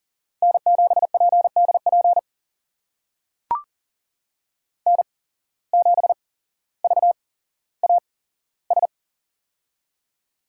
Morse code, heard as a keyed tone, about 40 wpm, spaced as for 17 wpm, 700 Hz. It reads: N7PDP N7VAS
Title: USA Call Signs - 5 Characters - 8x Character Spacing - 40wpm